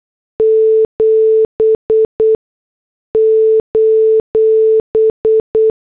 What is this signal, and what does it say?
Morse code, heard as a keyed tone, about 8 wpm, spaced as for 6 wpm, 440 Hz. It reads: 7: